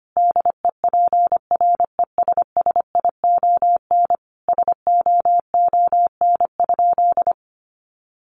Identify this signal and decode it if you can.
Morse code, heard as a keyed tone, about 25 wpm, 700 Hz. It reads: DEPRESSION SOON?
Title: Sentences from Top 1300 Words 25wpm